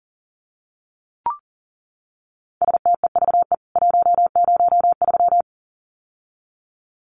Morse code, heard as a keyed tone, about 40 wpm, 700 Hz. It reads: STEVE 103
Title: CWT Exchange 40wpm